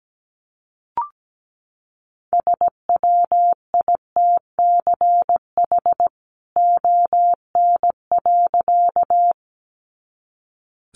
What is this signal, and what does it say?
Morse code, heard as a keyed tone, about 17 wpm, 700 Hz. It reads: SWITCH ON.
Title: A Sound Of Thunder 17wpm